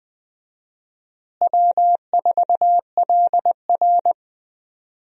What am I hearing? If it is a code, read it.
Morse code, heard as a keyed tone, about 20 wpm, 700 Hz. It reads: W4LR
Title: USA Call Signs - 4 Characters - Repeated 2x - 20wpm